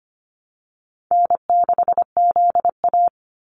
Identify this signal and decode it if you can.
Morse code, heard as a keyed tone, about 25 wpm, 690 Hz. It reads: N6ZA
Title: NA SOTA Activator Callsigns 25wpm